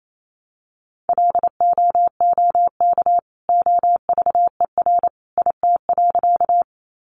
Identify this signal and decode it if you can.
Morse code, heard as a keyed tone, about 28 wpm, 700 Hz. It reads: LOOK OVER IT.
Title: Sentences from Top 100 Words 28wpm